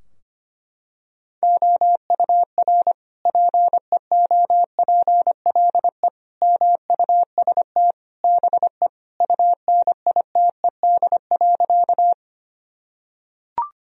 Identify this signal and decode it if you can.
Morse code, heard as a keyed tone, about 25 wpm, 700 Hz. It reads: OUR PEOPLE MUST BE UNITED.
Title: Sentences from Top 1000 Words - Review Part 4 - 25wpm